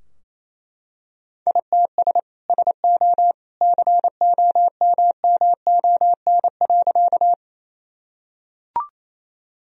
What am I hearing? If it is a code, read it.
Morse code, heard as a keyed tone, about 28 wpm, 700 Hz. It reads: ITS SO COMMON.
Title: Sentences from Top 1000 Words - Review Part 3 - 28wpm